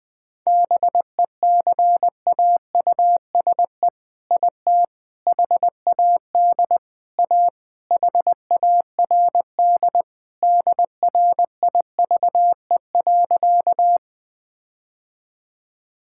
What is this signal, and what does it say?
Morse code, heard as a keyed tone, about 20 wpm, 700 Hz. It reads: BECAUSE IT HAD A HARD DRIVE.
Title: All-Age Appropriate Jokes - 20wpm